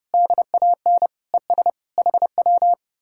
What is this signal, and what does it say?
Morse code, heard as a keyed tone, about 40 wpm, spaced as for 35 wpm, 700 Hz. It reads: DAN ES HW
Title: Speed-Racing Common QSO Phrases Encore - 30wpm